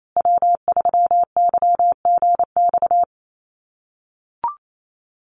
Morse code, heard as a keyed tone, about 28 wpm, 700 Hz. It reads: W3YGX
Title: USA Call Signs Repeated 3x 28wpm